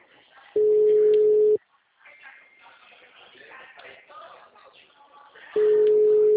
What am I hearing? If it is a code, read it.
Ringback tone (Europe, 425 Hz), with speech.
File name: out- -2040- - - .12